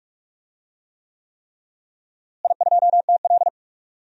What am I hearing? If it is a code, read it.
Morse code, heard as a keyed tone, about 46 wpm, 700 Hz. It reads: I2TL
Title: DXCC-100 Call Signs - 4 Characters - Repeated 3x - 45wpm